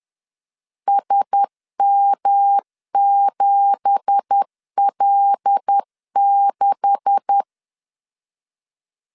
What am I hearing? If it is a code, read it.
Morse code, heard as a keyed tone, about 10 wpm, 800 Hz. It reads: SM7L6